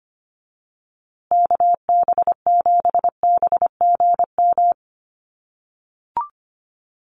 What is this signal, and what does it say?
Morse code, heard as a keyed tone, about 25 wpm, 700 Hz. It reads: KB7BGM
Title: USA Call Signs Repeated 3x 25wpm